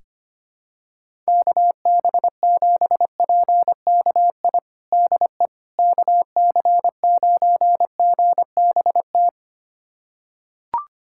Morse code, heard as a keyed tone, about 25 wpm, 700 Hz. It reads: KB7PKI DE KC9GBT
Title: USA Callsign DE Callsign - 6 Characters - 25wpm